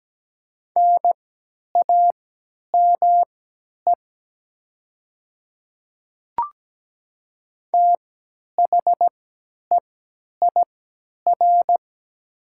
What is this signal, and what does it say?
Morse code, heard as a keyed tone, about 17 wpm, spaced as for 10 wpm, 700 Hz. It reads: NAME THEIR
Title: 3x Character Spacing - Top 200 Words - 17wpm